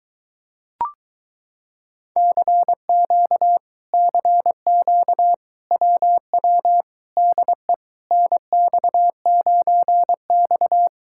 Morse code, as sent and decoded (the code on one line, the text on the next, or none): -.-. --.- / -.-. --.- / .-- .-- / -.. / -. -..- ----. -..-
CQ CQ WW DE NX9X